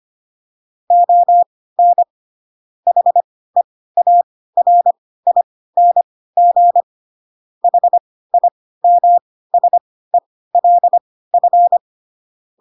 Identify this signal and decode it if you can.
Morse code, heard as a keyed tone, about 25 wpm, spaced as for 16 wpm, 700 Hz. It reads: ON HEARING HIMSELF
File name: Pinocchio / Pinocchio ch2 10 25 ik1hge